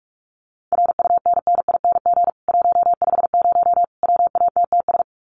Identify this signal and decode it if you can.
Morse code, heard as a keyed tone, about 44 wpm, 700 Hz. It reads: RUNNING 150 WATTS
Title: Common QSO Phrases 45wpm